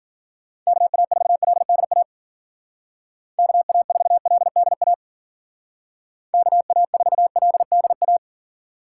Morse code, heard as a keyed tone, about 52 wpm, 700 Hz. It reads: KA4LDA KA4LDA KA4LDA